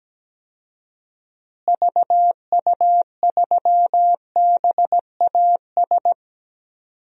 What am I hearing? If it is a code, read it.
Morse code, heard as a keyed tone, about 17 wpm, 700 Hz. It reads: VU3BAS
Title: DXCC-100 Call Signs - 6 Characters - Repeated 2x - 17wpm